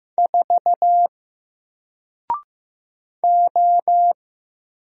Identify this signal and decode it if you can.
Morse code, heard as a keyed tone, about 15 wpm, 700 Hz. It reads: 4 O
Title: Single Letter-Number Alt 15wpm